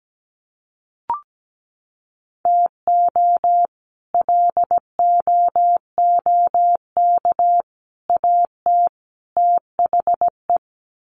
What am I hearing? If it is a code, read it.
Morse code, heard as a keyed tone, about 17 wpm, 700 Hz. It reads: TO LOOK AT THE
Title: Sets of 4 Words 17wpm